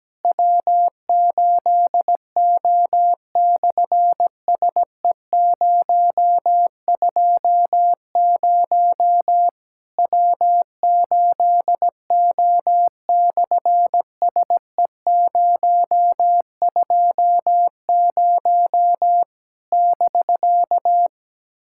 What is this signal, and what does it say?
Morse code, heard as a keyed tone, about 17 wpm, 700 Hz. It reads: W8O/SE020 W8O/SE020 <BK>
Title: SOTA QSO Practice 17wpm